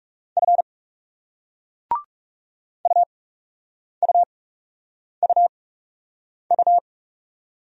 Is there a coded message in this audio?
Yes: Morse code from 0.4 to 6.8 s, heard as a keyed tone, about 38 wpm, spaced as for 8 wpm, 700 Hz. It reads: F UUUU